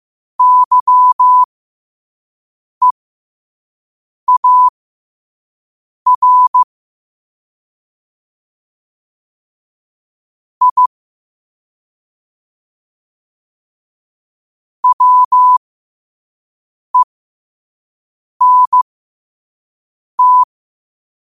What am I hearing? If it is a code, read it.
Morse code, heard as a keyed tone, about 15 wpm, spaced as for 5 wpm, 1000 Hz. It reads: YEAR I WENT
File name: 5wpm